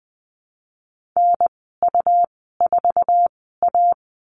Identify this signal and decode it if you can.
Morse code, heard as a keyed tone, about 19 wpm, spaced as for 14 wpm, 700 Hz. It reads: NU4A
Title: 161 Lesson 26g - Letter g - Call Signs 20wpm